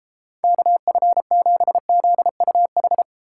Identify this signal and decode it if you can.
Morse code, heard as a keyed tone, about 29 wpm, 700 Hz. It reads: KF7ZUH